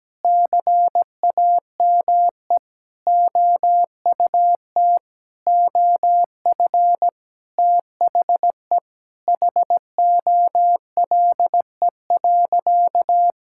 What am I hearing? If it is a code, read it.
Morse code, heard as a keyed tone, about 17 wpm, 700 Hz. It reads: CAME OUT OF THE HOLE.
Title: Sentences from Top 1600 Words 17wpm